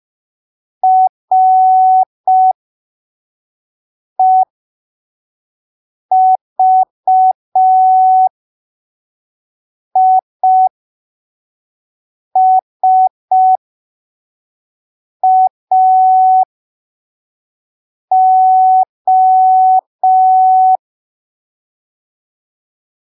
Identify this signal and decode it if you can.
Morse code, heard as a keyed tone, about 5 wpm, spaced as for 3 wpm, 750 Hz. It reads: REVISAO
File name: arquivos / texto / texto 22 05ppm